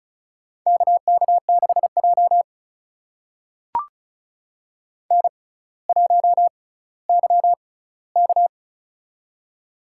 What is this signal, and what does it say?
Morse code, heard as a keyed tone, about 36 wpm, spaced as for 18 wpm, 700 Hz. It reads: KK6J N1YK
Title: USA Call Signs - 4 Characters - 6x Character Spacing - 35wpm